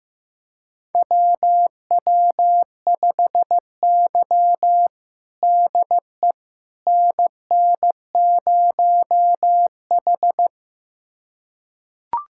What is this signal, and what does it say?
Morse code, heard as a keyed tone, about 15 wpm, 700 Hz. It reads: WW5Y DE NN0H
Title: USA Callsign DE Callsign - 4 Characters - 15wpm